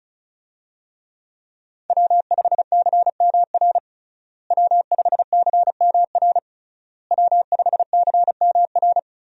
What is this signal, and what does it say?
Morse code, heard as a keyed tone, about 36 wpm, 700 Hz. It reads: W5CMR W5CMR W5CMR
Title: USA Call Signs - 5 Characters - Repeated 3x - 35wpm